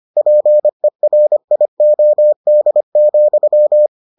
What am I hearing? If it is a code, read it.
Morse code, heard as a keyed tone, about 25 wpm, 600 Hz. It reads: PERIOD,